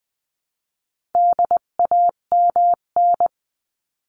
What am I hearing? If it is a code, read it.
Morse code, heard as a keyed tone, about 20 wpm, 700 Hz. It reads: DAMN
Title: 258 Lesson 42c - Words 16-20wpm